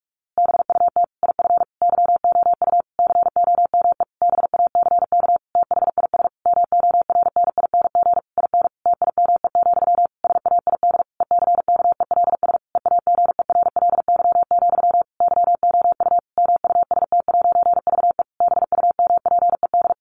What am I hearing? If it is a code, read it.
Morse code, heard as a keyed tone, about 44 wpm, 700 Hz. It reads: BUT IF YOU COME BACK THIS MORNING IN TIME, SAID ECKELS EAGERLY, YOU MUST'VE BUMPED
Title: A Sound Of Thunder 45wpm